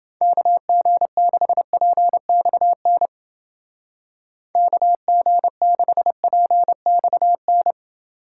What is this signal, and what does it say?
Morse code, heard as a keyed tone, about 28 wpm, 700 Hz. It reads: KG6PXN KG6PXN